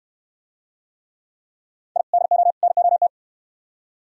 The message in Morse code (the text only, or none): I DO NOT